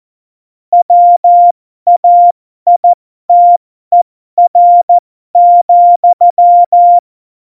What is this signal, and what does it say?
Morse code, heard as a keyed tone, about 14 wpm, spaced as for 12 wpm, 700 Hz. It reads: WAITER,
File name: Morsekurs / Witze